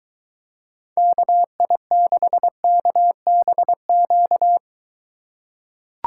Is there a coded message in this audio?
Yes: Morse code from 1.0 to 4.6 s, heard as a keyed tone, about 22 wpm, 700 Hz. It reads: KI6KBQ